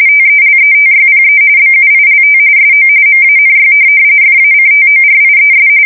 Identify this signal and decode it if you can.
RTTY, about 45.45 baud. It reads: TELETYPE BUILT SO MANY FACTORIES
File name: fsk170 / teletype-corp